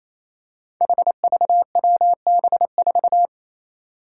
Morse code, heard as a keyed tone, about 27 wpm, 700 Hz. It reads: HVWB4
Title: Five Challenging Letter-Number Combinations 28wpm